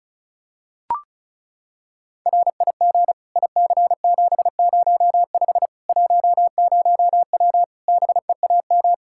Morse code, heard as a keyed tone, about 36 wpm, 700 Hz. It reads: RIG IC705 10W BEAM
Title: Practice QSO - Exchange 2 - Info - 35wpm